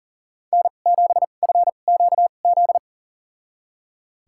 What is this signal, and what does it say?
Morse code, heard as a keyed tone, about 39 wpm, spaced as for 29 wpm, 700 Hz. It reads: N7FQZ
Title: USA Call Signs - 2x Character Spacing - 40wpm